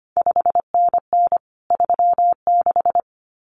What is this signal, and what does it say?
Morse code, heard as a keyed tone, about 25 wpm, 700 Hz. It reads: 5NN 36
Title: CQ WW DX Contest Exchange - 25wpm